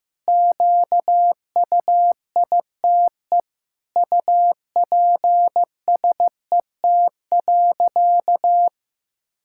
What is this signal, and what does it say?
Morse code, heard as a keyed tone, about 16 wpm, 700 Hz. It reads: QUITE UPSET.
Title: Sentences from Top 1000 Words - Review Part 5 - 15wpm